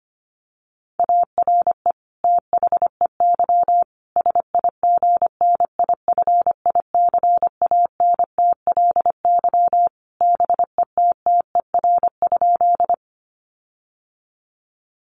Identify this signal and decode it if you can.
Morse code, heard as a keyed tone, about 25 wpm, 700 Hz. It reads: ARE THEY SIGNIFICANTLY BETTER?
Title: Sentences from Top 1500 Words 25wpm